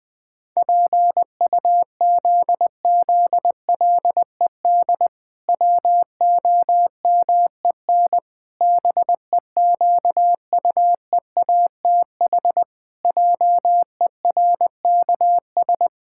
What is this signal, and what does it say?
Morse code, heard as a keyed tone, about 20 wpm, 700 Hz. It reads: PUZZLED WOMEN BEQUEATH JERKS
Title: Pangram Challenge v2 - 20wpm